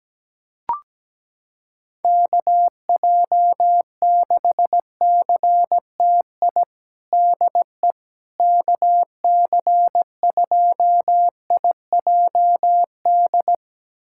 Morse code, heard as a keyed tone, about 17 wpm, 700 Hz. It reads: KJ6CTI DE KC2IJD
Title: USA Callsign DE Callsign - 6 Characters - 17wpm